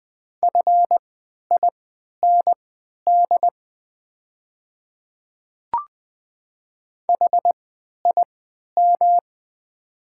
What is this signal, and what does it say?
Morse code, heard as a keyed tone, about 20 wpm, spaced as for 11 wpm, 700 Hz. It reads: FIND HIM
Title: 3x Character Spacing - Top 500 Words - 20wpm